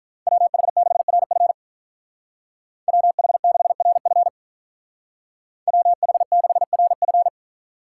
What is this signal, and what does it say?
Morse code, heard as a keyed tone, about 47 wpm, 700 Hz. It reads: WH6RF WH6RF WH6RF